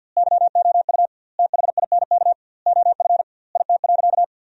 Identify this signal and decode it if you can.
Morse code, heard as a keyed tone, about 50 wpm, 700 Hz. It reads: YOU THINK OF IT.
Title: 2x Word-Spacing - Sentences from Top 200 Words - 50wpm